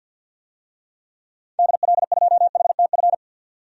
Morse code, heard as a keyed tone, about 52 wpm, 700 Hz. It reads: DL2HTF
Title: DXCC-100 Call Signs - 6 Characters - 50wpm